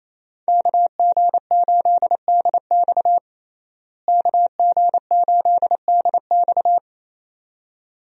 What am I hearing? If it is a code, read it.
Morse code, heard as a keyed tone, about 27 wpm, 700 Hz. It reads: KG8DX KG8DX
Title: USA Call Signs - 5 Characters - Repeated 2x - 28wpm